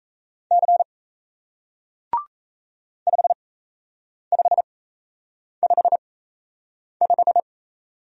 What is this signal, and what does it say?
Morse code, heard as a keyed tone, about 37 wpm, spaced as for 8 wpm, 700 Hz. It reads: C 5555